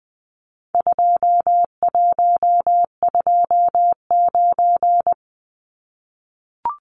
Morse code, heard as a keyed tone, about 20 wpm, 700 Hz. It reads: 2129
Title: Numbers - 4-Digit 20wpm